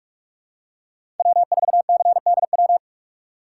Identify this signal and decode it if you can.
Morse code, heard as a keyed tone, about 47 wpm, 700 Hz. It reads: W4CDW
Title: USA Call Signs - 5 Characters - 6x Character Spacing - 45wpm